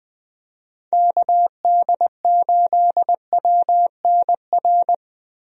Morse code, heard as a keyed tone, about 20 wpm, 700 Hz. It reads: KD8WNR
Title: USA Call Signs - 6 Characters - 20wpm